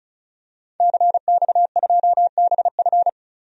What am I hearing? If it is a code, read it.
Morse code, heard as a keyed tone, about 34 wpm, 700 Hz. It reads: CX2BF
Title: DXCC-100 Call Signs - 5 Characters - 35wpm